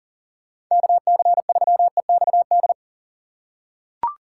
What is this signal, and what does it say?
Morse code, heard as a keyed tone, about 40 wpm, 700 Hz. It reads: KC3EXD
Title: USA Call Signs - 6 Characters - 4x Character Spacing - 40wpm